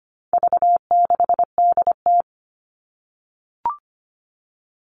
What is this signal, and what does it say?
Morse code, heard as a keyed tone, about 25 wpm, 700 Hz. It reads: V6DT
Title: All Cut Numbers - 4-Digit 25wpm